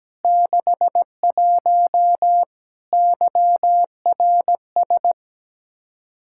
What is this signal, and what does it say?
Morse code, heard as a keyed tone, about 17 wpm, 700 Hz. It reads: 61 YRS
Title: Practice QSO - Exchange 3 - More Info - 17wpm